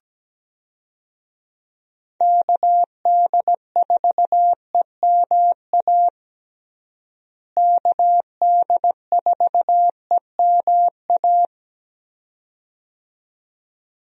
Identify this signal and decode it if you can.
Morse code, heard as a keyed tone, about 17 wpm, 700 Hz. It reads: KD4EMA KD4EMA
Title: USA Call Signs - 6 Characters - Repeated 2x - 17wpm